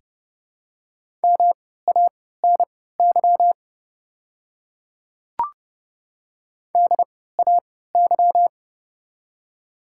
Morse code, heard as a keyed tone, about 30 wpm, spaced as for 17 wpm, 700 Hz. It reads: MANY DAY
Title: 3x Character Spacing - Top 100 Words - 30wpm